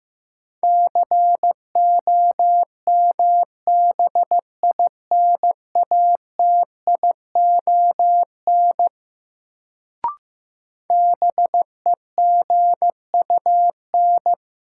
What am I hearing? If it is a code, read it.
Morse code, heard as a keyed tone, about 15 wpm, 700 Hz. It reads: COMBINATION BEGUN
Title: Sentences from Top 1000 Words - Review Part 6 - New Word Drill - 15wpm